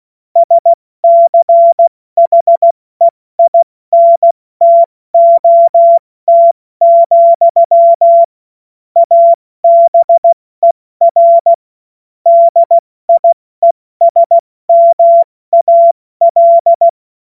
Morse code, heard as a keyed tone, about 16 wpm, spaced as for 14 wpm, 680 Hz. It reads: SCHEINTOT, ABER DIESMAL
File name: Morsekurs / Witze / Witz 25. Januar 2022, 70 F80